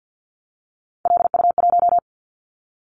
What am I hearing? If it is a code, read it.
Morse code, heard as a keyed tone, about 48 wpm, 700 Hz. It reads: LU1
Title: Callsign Prefixes for Top 100 DXCC - 50wpm